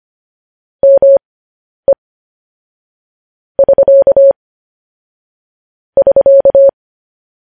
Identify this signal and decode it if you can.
Morse code, heard as a keyed tone, about 25 wpm, spaced as for 10 wpm, 550 Hz. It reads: ME <SK> <SK>
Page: News Headlines in Morse Code at 10 WPM